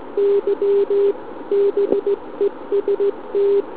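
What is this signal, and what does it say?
Morse code, heard as a keyed tone, about 17 wpm, 400 Hz. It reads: Y BEST
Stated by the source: SM7ALC heard via tropo on 144 MHz with an FT817 and a quarter-wave antenna on a central-heating radiator.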